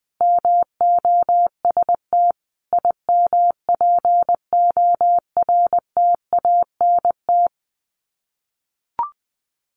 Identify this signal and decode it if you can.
Morse code, heard as a keyed tone, about 20 wpm, 700 Hz. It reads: MOST IMPORTANT
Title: Sets of 5 Words 20wpm